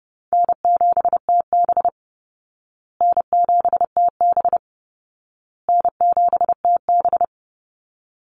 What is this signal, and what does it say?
Morse code, heard as a keyed tone, about 30 wpm, 700 Hz. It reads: N7TB N7TB N7TB
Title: Easy Call Signs Repeated 3x - 30wpm